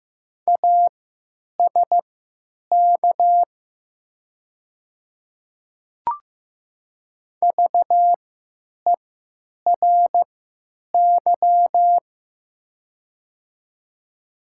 Morse code, heard as a keyed tone, about 15 wpm, spaced as for 9 wpm, 690 Hz. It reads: ASK VERY
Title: 3x Character Spacing - Top 300 Words - 15wpm